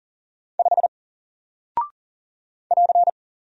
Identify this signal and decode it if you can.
Morse code, heard as a keyed tone, about 40 wpm, 700 Hz. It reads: 5 <AR>
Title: Single Character (Letter:Number:Punctuation:Prosign) Alt 40wpm